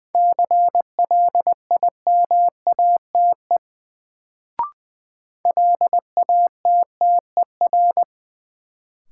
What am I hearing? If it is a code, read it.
Morse code, heard as a keyed tone, about 21 wpm, 700 Hz. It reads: CLIMATE LATTER
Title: Sentences from Top 1600 Words - New Word Drill Alt - 20wpm